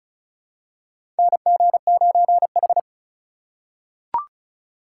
Morse code, heard as a keyed tone, about 34 wpm, 700 Hz. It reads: NG9H
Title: USA Call Signs - 4 Characters - 4x Character Spacing - 35wpm